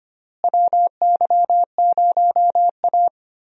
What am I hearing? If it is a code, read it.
Morse code, heard as a keyed tone, about 25 wpm, 700 Hz. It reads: WY0A
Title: USA Call Signs - 4 Characters - Repeated 2x - 25wpm